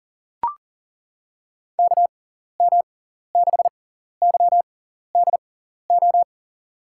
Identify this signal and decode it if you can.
Morse code, heard as a keyed tone, about 40 wpm, spaced as for 14 wpm, 700 Hz. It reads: KM6YDO
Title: USA Call Signs - 6 Characters - 6x Character Spacing - 40wpm